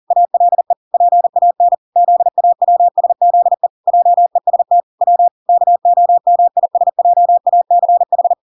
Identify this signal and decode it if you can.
Morse code, heard as a keyed tone, about 40 wpm, 700 Hz. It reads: ALE PAN ZAWSZE JEST W KOMISJACH